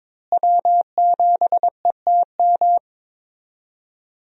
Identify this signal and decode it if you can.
Morse code, heard as a keyed tone, about 22 wpm, 700 Hz. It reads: W7ETM
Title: USA Call Signs - 5 Characters - Repeated 2x - 22wpm